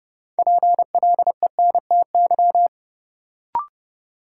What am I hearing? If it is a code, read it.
Morse code, heard as a keyed tone, about 30 wpm, 700 Hz. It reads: PLENTY
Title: Sentences from Top 1300 Words - New Word Drill - 30wpm